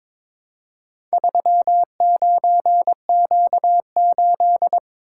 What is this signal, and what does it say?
Morse code, heard as a keyed tone, about 22 wpm, 700 Hz. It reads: 39Q8
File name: practice / 22wpm / Four Letter-Number Combinations - 22wpm